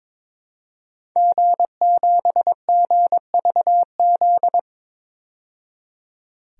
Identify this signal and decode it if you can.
Morse code, heard as a keyed tone, about 22 wpm, 700 Hz. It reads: G7GVZ
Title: DXCC-100 Call Signs - 5 Characters - 22wpm